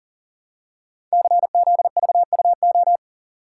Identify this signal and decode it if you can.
Morse code, heard as a keyed tone, about 39 wpm, 700 Hz. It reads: CZVUO
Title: Code Groups - 5 Letter:Numbers - 40wpm